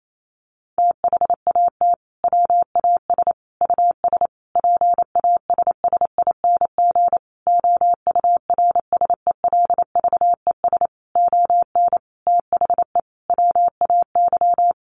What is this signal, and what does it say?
Morse code, heard as a keyed tone, about 28 wpm, 700 Hz. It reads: THAT WAS US PASSING OURSELVES ON THE WAY
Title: A Sound Of Thunder 28wpm